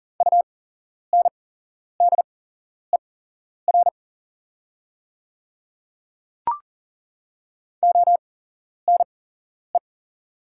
Morse code, heard as a keyed tone, about 39 wpm, spaced as for 11 wpm, 700 Hz. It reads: UNDER ONE
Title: 8x Character Spacing - Top 300 Words - 40wpm